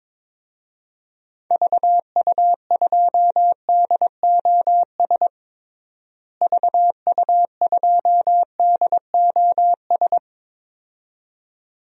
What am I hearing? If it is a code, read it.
Morse code, heard as a keyed tone, about 22 wpm, 700 Hz. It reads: VU2DOS VU2DOS
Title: DXCC-100 Call Signs - 6 Characters - Repeated 2x - 22wpm